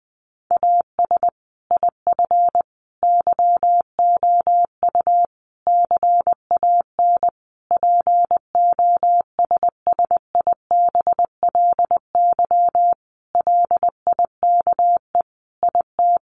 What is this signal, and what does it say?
Morse code, heard as a keyed tone, about 20 wpm, 700 Hz. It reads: AS IF YOU CAN POSSIBLY LIKE IT
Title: Sentences from Top 1300 Words 20wpm